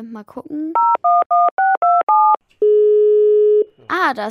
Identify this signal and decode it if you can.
DTMF, heard as telephone touch tones, tones of about 210 ms, with speech. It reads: 011627